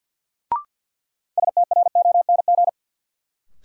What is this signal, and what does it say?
Morse code, heard as a keyed tone, about 47 wpm, 700 Hz. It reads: STRONG
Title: Sentences from Top 500 Words - New Word Drill - 50wpm